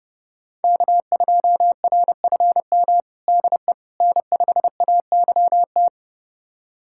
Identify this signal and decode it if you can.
Morse code, heard as a keyed tone, about 30 wpm, 700 Hz. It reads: K2RFM DE N5AYT
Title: USA Callsign DE Callsign - 5 Characters - 30wpm